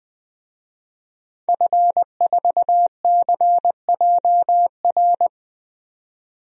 Morse code, heard as a keyed tone, about 21 wpm, 700 Hz. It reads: F4CJR
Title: DXCC-100 Call Signs - 5 Characters - 20wpm